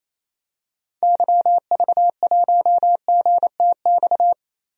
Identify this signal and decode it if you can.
Morse code, heard as a keyed tone, about 28 wpm, 700 Hz. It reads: YV1GTX
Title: DXCC-100 Call Signs - 6 Characters - Repeated 2x - 28wpm